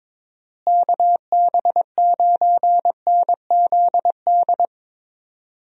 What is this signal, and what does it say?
Morse code, heard as a keyed tone, about 21 wpm, 700 Hz. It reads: KB9NZD
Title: USA Call Signs - 6 Characters - Repeated 2x - 22wpm